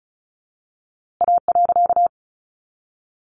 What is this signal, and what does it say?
Morse code, heard as a keyed tone, about 34 wpm, 700 Hz. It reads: A.
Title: A Sound Of Thunder 35wpm